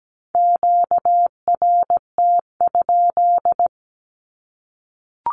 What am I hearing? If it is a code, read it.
Morse code, heard as a keyed tone, about 17 wpm, 700 Hz. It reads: QRT?
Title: Q-Codes Repeated 3x 17wpm